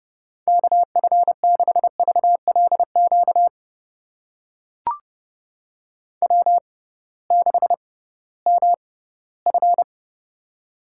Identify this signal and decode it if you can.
Morse code, heard as a keyed tone, about 29 wpm, spaced as for 17 wpm, 700 Hz. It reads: KF6VLQ W6MF
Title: USA Call Signs - 6x Character Spacing - 30wpm